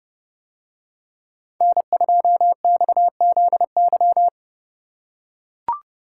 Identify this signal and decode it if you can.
Morse code, heard as a keyed tone, about 29 wpm, 700 Hz. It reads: N2XZY